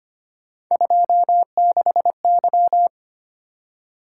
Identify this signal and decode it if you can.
Morse code, heard as a keyed tone, about 25 wpm, 700 Hz. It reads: 26Y
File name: Three Letter-Number Combinations - 25wpm